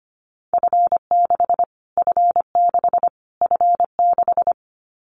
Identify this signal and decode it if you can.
Morse code, heard as a keyed tone, about 25 wpm, 700 Hz. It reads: F6 F6 F6
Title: Callsign Prefixes for Top 25 DXCC - Repeated 3x - 25wpm